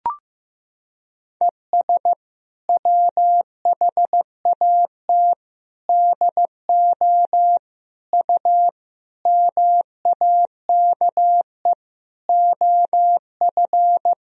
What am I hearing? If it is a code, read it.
Morse code, heard as a keyed tone, about 15 wpm, 700 Hz. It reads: ES WHAT DO U MAKE OF
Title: Sentences from Top 100 Words and Abbreviations 15wpm